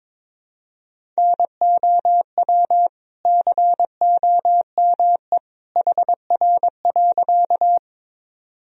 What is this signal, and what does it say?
Morse code, heard as a keyed tone, about 22 wpm, 700 Hz. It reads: NOW COME HR.
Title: Sentences from Top 200 Words and Abbreviations 22wpm